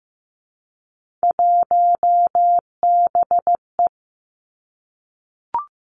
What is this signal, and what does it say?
Morse code, heard as a keyed tone, about 15 wpm, 700 Hz. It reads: 1BE
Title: Three Challenging Letter-Number Combinations 15wpm